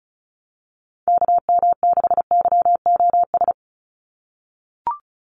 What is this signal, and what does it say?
Morse code, heard as a keyed tone, about 34 wpm, 700 Hz. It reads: KM6YOS
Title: USA Call Signs 35wpm